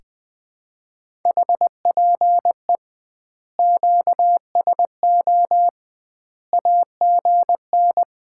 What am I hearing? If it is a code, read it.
Morse code, heard as a keyed tone, about 20 wpm, 700 Hz. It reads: HPE QSO AGN
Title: Common QSO Phrases - 2x Word-Spacing - 20wpm